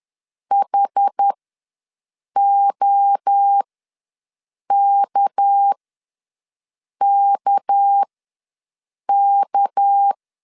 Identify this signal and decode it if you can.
Morse code, heard as a keyed tone, about 10 wpm, spaced as for 6 wpm, 800 Hz. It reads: HOKKK
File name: LEZIONI / 6 PKX0 / 6 PK ws 10wpm 3745